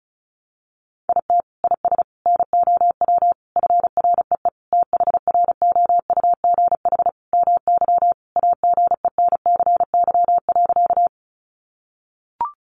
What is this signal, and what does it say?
Morse code, heard as a keyed tone, about 34 wpm, 700 Hz. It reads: IT IS NOW FREE THROUGH MY AGENCY.
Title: Sentences from Top 1300 Words 35wpm